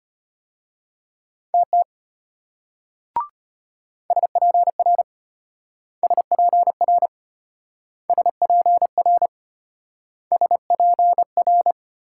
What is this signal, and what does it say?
Morse code, heard as a keyed tone, about 33 wpm, 700 Hz. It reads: TT SPR SPR SPR SPR